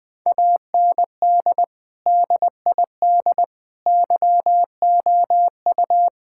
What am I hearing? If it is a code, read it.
Morse code, heard as a keyed tone, about 20 wpm, 700 Hz. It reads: AND DID YOU
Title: Sentences from Top 1000 Words - Review Part 6 - 20wpm